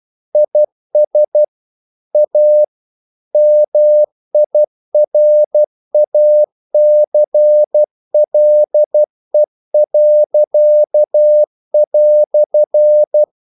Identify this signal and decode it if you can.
Morse code, heard as a keyed tone, about 12 wpm, 600 Hz. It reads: IS A MIRACLE."
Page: English quotations in CW (telegraphy - morse)